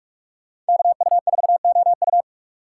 Morse code, heard as a keyed tone, about 59 wpm, 700 Hz. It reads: KU4OU